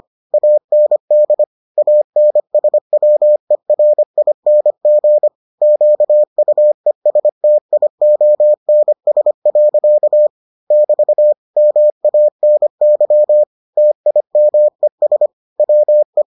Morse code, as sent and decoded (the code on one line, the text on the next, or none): .- -. -.. / .- -. ... .-- . .-. .. -. --. / --.- ..- . ... - .. --- -. ... .-.-.- / -...- / -- .- -. -.-- / - .. -- . ... / .-- .
AND ANSWERING QUESTIONS. <BT> MANY TIMES WE